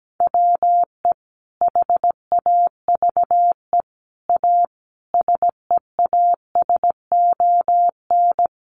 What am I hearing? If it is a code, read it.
Morse code, heard as a keyed tone, about 17 wpm, 700 Hz. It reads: WE HAVE A SEASON
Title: Sentences from Top 1400 Words 17wpm